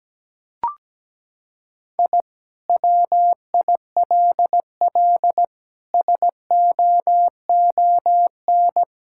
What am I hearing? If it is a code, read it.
Morse code, heard as a keyed tone, about 17 wpm, 700 Hz. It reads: I WILL SOON
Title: Sentences from Top 1000 Words - Review Part 6 - 17wpm